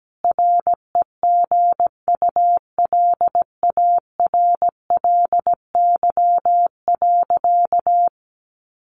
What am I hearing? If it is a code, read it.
Morse code, heard as a keyed tone, about 17 wpm, 700 Hz. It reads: REGULARLY.
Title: Sentences from Top 1400 Words 17wpm